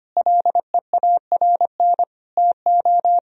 Morse code, heard as a keyed tone, about 25 wpm, 700 Hz. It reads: LEARN TO
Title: Sentences from Top 1000 Words - Review Part 4 - 25wpm